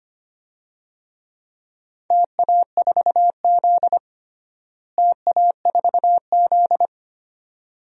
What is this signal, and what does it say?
Morse code, heard as a keyed tone, about 25 wpm, 700 Hz. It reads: TA4Z TA4Z
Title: DXCC-100 Call Signs - 4 Characters - Repeated 2x - 25wpm